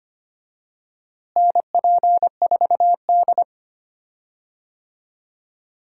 Morse code, heard as a keyed tone, about 25 wpm, 700 Hz. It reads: NP4D
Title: USA Call Signs - 4 Characters - Repeated 3x - 25wpm